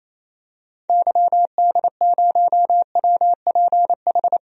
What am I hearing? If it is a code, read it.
Morse code, heard as a keyed tone, about 28 wpm, 700 Hz. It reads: YD0WPH
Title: DXCC-100 Call Signs - 6 Characters - Repeated 3x - 28wpm